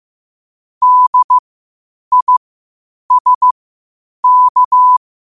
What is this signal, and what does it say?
Morse code, heard as a keyed tone, about 15 wpm, spaced as for 8 wpm, 1000 Hz. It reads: DISK
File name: qotdfiles / 8wpm